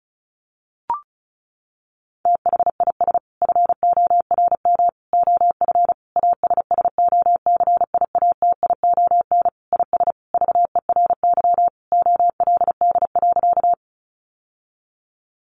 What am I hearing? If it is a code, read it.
Morse code, heard as a keyed tone, about 34 wpm, 700 Hz. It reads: THIS FORM OF ASSOCIATION IS VERY OLD.
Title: Sentences from Top 1100 Words 35wpm